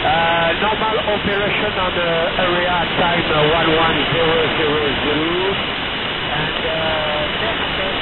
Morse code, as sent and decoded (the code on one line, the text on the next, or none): none